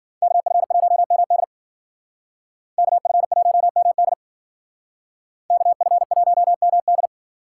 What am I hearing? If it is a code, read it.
Morse code, heard as a keyed tone, about 50 wpm, 700 Hz. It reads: KF1MD KF1MD KF1MD